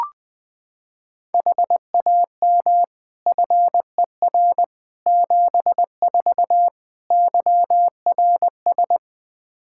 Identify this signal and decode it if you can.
Morse code, heard as a keyed tone, about 20 wpm, 700 Hz. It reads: HAM FER 74 YRS
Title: Practice QSO - Exchange 3 - More Info - 20wpm